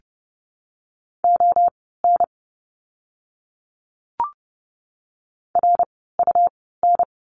Morse code, heard as a keyed tone, about 30 wpm, spaced as for 17 wpm, 700 Hz. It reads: ON RUN
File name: Lesson 17f - Letter u - Words 30wpm